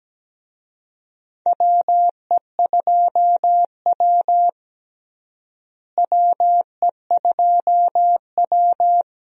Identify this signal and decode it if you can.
Morse code, heard as a keyed tone, about 17 wpm, 700 Hz. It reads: WE2W WE2W
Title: USA Call Signs - 4 Characters - Repeated 2x - 17wpm